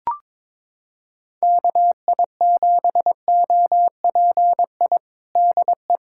Morse code, heard as a keyed tone, about 22 wpm, 700 Hz. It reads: KI7OPI DE
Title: USA Callsign DE Callsign 22wpm